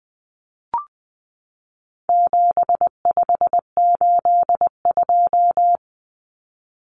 Morse code, heard as a keyed tone, about 20 wpm, 700 Hz. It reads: 7582
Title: Numbers - 4-Digit 20wpm